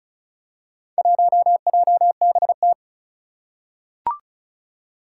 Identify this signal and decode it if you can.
Morse code, heard as a keyed tone, about 36 wpm, 700 Hz. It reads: 1JBT
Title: Four Challenging Letter-Number Combinations 35wpm